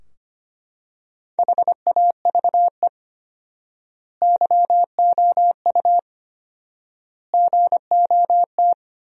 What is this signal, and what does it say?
Morse code, heard as a keyed tone, about 25 wpm, 700 Hz. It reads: HAVE YOU GOT
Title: 4x Word-Spacing - Sentences from Top 300 Words - 25wpm